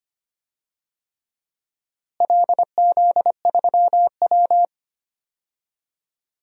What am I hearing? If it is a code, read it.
Morse code, heard as a keyed tone, about 25 wpm, 700 Hz. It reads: LZ3W
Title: DXCC-100 Call Signs - 4 Characters - Repeated 3x - 25wpm